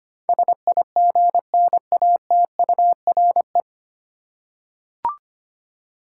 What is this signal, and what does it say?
Morse code, heard as a keyed tone, about 25 wpm, 700 Hz. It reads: SIGNATURE
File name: Nine Letter Words - 25wpm